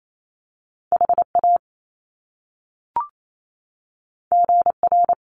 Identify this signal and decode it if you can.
Morse code, heard as a keyed tone, about 28 wpm, 700 Hz. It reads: HA GR
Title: Top Two Letter Combinations in Words 28wpm